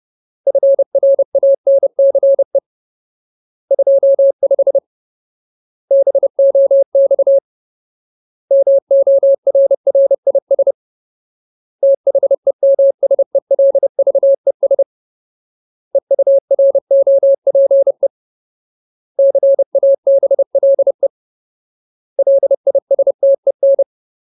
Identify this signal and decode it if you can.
Morse code, heard as a keyed tone, about 30 wpm, 550 Hz. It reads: FRANCE 25 BOX MORRIS THEMSELVES EUROPE CABLE LISTEN
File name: Ultimate Word List / 30WPM Ultimate